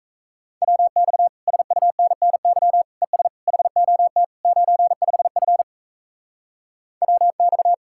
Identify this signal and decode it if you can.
Morse code, heard as a keyed tone, about 37 wpm, 700 Hz. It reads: WX SUNNY ES HOT 95F WX